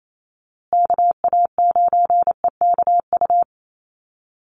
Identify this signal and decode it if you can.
Morse code, heard as a keyed tone, about 28 wpm, 700 Hz. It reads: KA9EKU
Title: USA Call Signs Repeated 2x 28wpm